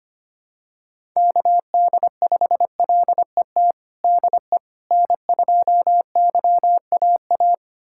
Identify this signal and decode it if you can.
Morse code, heard as a keyed tone, about 25 wpm, 700 Hz. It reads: KD5LET DE N2YAA
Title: USA Callsign DE Callsign 25wpm